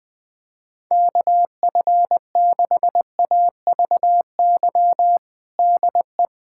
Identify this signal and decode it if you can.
Morse code, heard as a keyed tone, about 20 wpm, 700 Hz. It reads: KF6AVY DE
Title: USA Callsign DE Callsign - 6 Characters - 20wpm